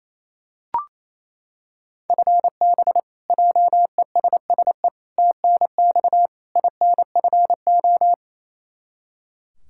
Morse code, heard as a keyed tone, about 28 wpm, 700 Hz. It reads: FB JESSE TNX INFO
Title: Practice QSO - Exchange 3 - More Info - 28wpm